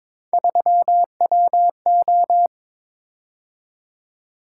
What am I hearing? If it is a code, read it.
Morse code, heard as a keyed tone, about 22 wpm, 700 Hz. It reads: 3WO
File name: practice / 22wpm / Three Letter-Number Combinations - 22wpm